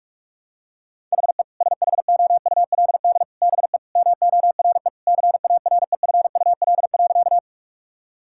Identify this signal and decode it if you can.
Morse code, heard as a keyed tone, about 47 wpm, 700 Hz. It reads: HE SHOULD BE MORE CAREFUL.
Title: Sentences from Top 1600 Words 45wpm